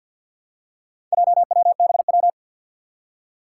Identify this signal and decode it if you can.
Morse code, heard as a keyed tone, about 48 wpm, 700 Hz. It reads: JWBW
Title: Four Challenging Letter-Number Combinations 50wpm